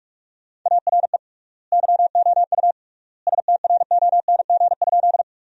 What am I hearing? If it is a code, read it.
Morse code, heard as a keyed tone, about 46 wpm, 700 Hz. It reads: ARE YOU STRONG?
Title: 3x Word-Spacing - Sentences from Top 500 Words - 45wpm